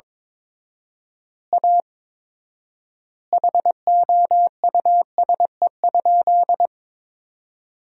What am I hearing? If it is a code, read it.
Morse code, heard as a keyed tone, about 22 wpm, 700 Hz. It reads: A HOUSE?
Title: 4x Word-Spacing - Sentences from Top 200 Words - 22wpm